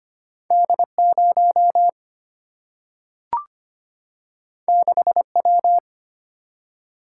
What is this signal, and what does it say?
Morse code, heard as a keyed tone, about 25 wpm, 700 Hz. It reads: D 0 6 W
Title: Single-Letter Single-Number 25wpm